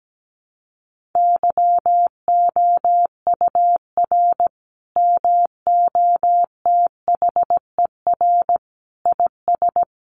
Morse code, heard as a keyed tone, about 17 wpm, 700 Hz. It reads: YOUR MOTHER IS
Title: Sentences from Top 1400 Words 17wpm